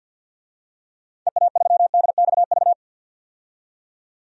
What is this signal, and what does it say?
Morse code, heard as a keyed tone, about 49 wpm, 700 Hz. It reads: EA3DXV
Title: DXCC-100 Call Signs - 6 Characters - 50wpm